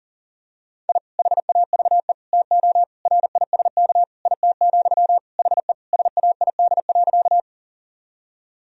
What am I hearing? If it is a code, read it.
Morse code, heard as a keyed tone, about 40 wpm, 700 Hz. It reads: I HAVE TO RISK IT, HE SAID.
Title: Sentences from Top 1000 Words - Review Part 4 - 40wpm